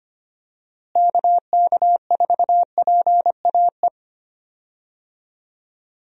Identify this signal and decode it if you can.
Morse code, heard as a keyed tone, about 25 wpm, 700 Hz. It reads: KK4PAE
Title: USA Call Signs - 6 Characters - Repeated 3x - 25wpm